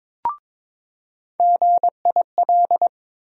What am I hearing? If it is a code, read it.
Morse code, heard as a keyed tone, about 22 wpm, 700 Hz. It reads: GIL
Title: USA Call Sign Postfixes - 3 Letters 22wpm